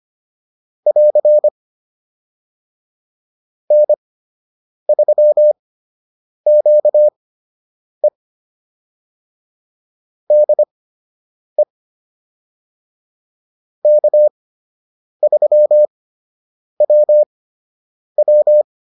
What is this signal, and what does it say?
Morse code, heard as a keyed tone, about 25 wpm, spaced as for 8 wpm, 600 Hz. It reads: <AR> N3QE DE K3WW
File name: QSO 101 08